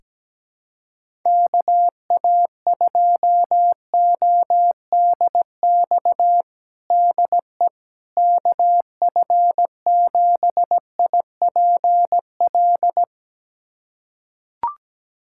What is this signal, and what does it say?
Morse code, heard as a keyed tone, about 17 wpm, 700 Hz. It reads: KA2ODX DE KF7IPL
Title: USA Callsign DE Callsign - 6 Characters - 17wpm